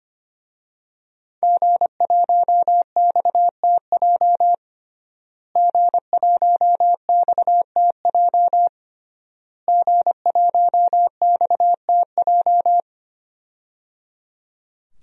Morse code, heard as a keyed tone, about 25 wpm, 700 Hz. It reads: G1XTJ G1XTJ G1XTJ
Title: DXCC-100 Call Signs - 5 Characters - Repeated 3x - 25wpm